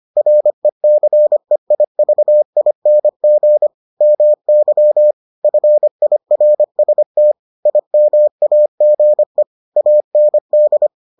Morse code, heard as a keyed tone, about 25 wpm, 600 Hz. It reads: RECEIVING MY FIRST IMAGE AND